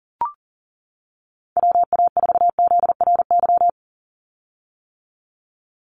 Morse code, heard as a keyed tone, about 40 wpm, 700 Hz. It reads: WA4ZRY
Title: USA Call Signs 40wpm